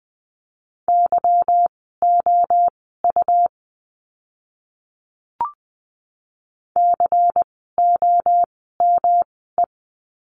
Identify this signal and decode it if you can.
Morse code, heard as a keyed tone, about 20 wpm, spaced as for 14 wpm, 700 Hz. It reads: YOU COME
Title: 237 Lesson 38f - Letter q - Words 20wpm